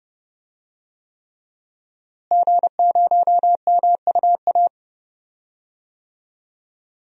Morse code, heard as a keyed tone, about 31 wpm, 700 Hz. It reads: G0MUA
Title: DXCC-100 Call Signs - 5 Characters - Repeated 3x - 30wpm